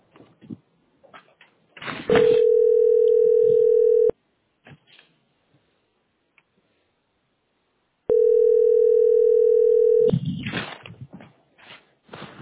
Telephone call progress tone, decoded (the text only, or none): ringback tone